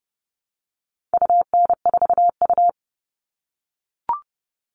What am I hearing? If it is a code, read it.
Morse code, heard as a keyed tone, about 30 wpm, 700 Hz. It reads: UN4U
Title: All Cut Numbers - 4-Digit 30wpm